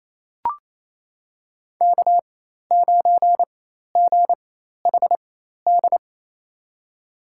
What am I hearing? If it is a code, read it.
Morse code, heard as a keyed tone, about 28 wpm, spaced as for 13 wpm, 700 Hz. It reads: K9GHD
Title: USA Call Signs - 5 Characters - 4x Character Spacing - 28wpm